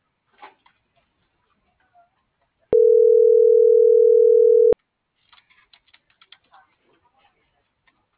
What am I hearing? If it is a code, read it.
Ringback tone (North America), 440 + 480 Hz.